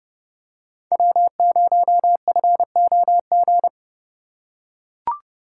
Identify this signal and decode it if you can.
Morse code, heard as a keyed tone, about 31 wpm, 700 Hz. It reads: W0FOG
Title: USA Call Signs - 5 Characters - 3x Character Spacing - 30wpm